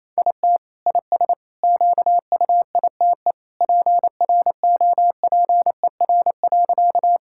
Morse code, heard as a keyed tone, about 28 wpm, 700 Hz. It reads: IT IS QUITE PROPER.
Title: Sentences from Top 1000 Words - Review Part 5 - 28wpm